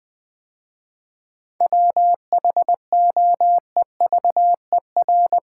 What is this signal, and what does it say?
Morse code, heard as a keyed tone, about 20 wpm, 700 Hz. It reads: WHOEVER
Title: 7 Letter Words - First 3 Letters Given - 20wpm